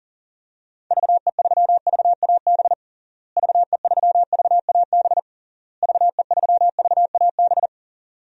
Morse code, heard as a keyed tone, about 37 wpm, 700 Hz. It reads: VE3VAB VE3VAB VE3VAB
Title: DXCC-100 Call Signs - 6 Characters - Repeated 3x - 40wpm